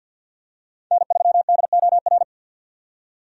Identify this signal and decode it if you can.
Morse code, heard as a keyed tone, about 46 wpm, 700 Hz. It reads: N3DOR
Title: USA Call Signs - 5 Characters - 50wpm